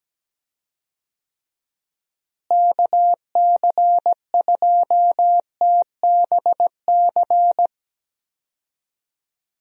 Morse code, heard as a keyed tone, about 17 wpm, 700 Hz. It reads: KC2TBC
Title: USA Call Signs - 6 Characters - Repeated 3x - 17wpm